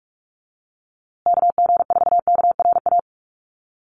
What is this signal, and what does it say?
Morse code, heard as a keyed tone, about 44 wpm, 700 Hz. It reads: KG4KRA
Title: USA Call Signs Repeated 2x 45wpm